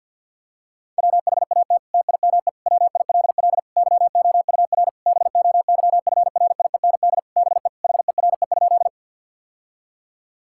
Morse code, heard as a keyed tone, about 50 wpm, 700 Hz. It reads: WHAT TIME WILL YOUR BOYFRIEND BE HERE?
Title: Sentences from Top 1000 Words - Review Part 5 - 50wpm